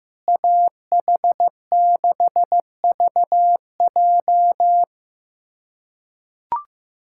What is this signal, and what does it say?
Morse code, heard as a keyed tone, about 15 wpm, 700 Hz. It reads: AH6VJ
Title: USA Callsign DE Callsign - 5 Characters - 15wpm